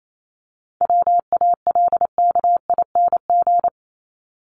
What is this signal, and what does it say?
Morse code, heard as a keyed tone, about 28 wpm, 700 Hz. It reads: WALKING
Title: ING Suffix - Top 100 Words with Base Words 28wpm